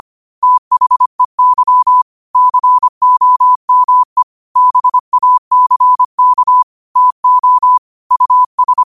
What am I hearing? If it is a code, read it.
Morse code, heard as a keyed tone, about 25 wpm, 1000 Hz. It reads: THEY COME BACK TO US
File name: qotdfiles / 25wpm